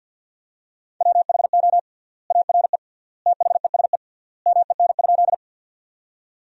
Morse code, heard as a keyed tone, about 51 wpm, 700 Hz. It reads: WHO ARE THESE MEN?
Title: 3x Word-Spacing - Sentences from Top 200 Words - 50wpm